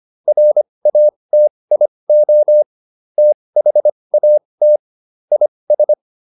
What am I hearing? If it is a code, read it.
Morse code, heard as a keyed tone, about 25 wpm, spaced as for 20 wpm, 600 Hz. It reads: RATIO THAT IS